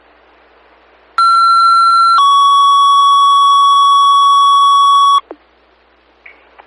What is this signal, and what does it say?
Two-tone page, heard as a one-second tone, then a three-second tone, about 1360 then 1120 Hz. It is heard from 1.2 to 5.2 s.